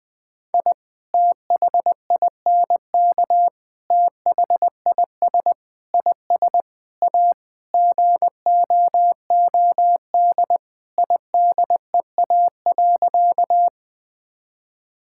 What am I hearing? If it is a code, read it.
Morse code, heard as a keyed tone, about 21 wpm, 700 Hz. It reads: I THINK THIS IS A GOOD IDEA.
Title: Sentences with 5-Word Phrases - Spoken Beginning - 20wpm